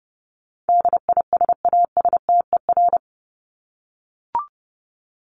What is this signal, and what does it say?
Morse code, heard as a keyed tone, about 30 wpm, 700 Hz. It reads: DISASTER
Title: Top 1501-1750 Words 30wpm